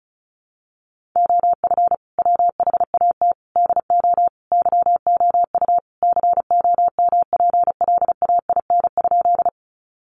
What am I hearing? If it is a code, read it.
Morse code, heard as a keyed tone, about 34 wpm, 700 Hz. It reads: OF WHAT DO YOU COMPLAIN?
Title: Sentences from Top 1300 Words 35wpm